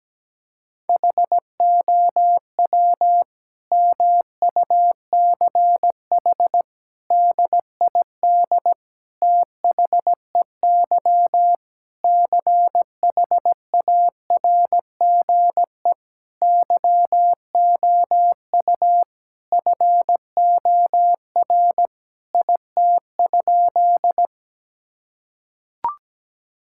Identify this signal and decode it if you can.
Morse code, heard as a keyed tone, about 17 wpm, 700 Hz. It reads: HOW MUCH DID THEY CHARGE YOU FOR IT?
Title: Sentences from Top 1000 Words - Review Part 6 - 17wpm